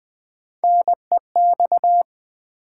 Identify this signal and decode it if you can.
Morse code, heard as a keyed tone, about 20 wpm, 700 Hz. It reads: NEX
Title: Build A Word - 1 Letter At A Time - 3 to 5-Letter Words - 20wpm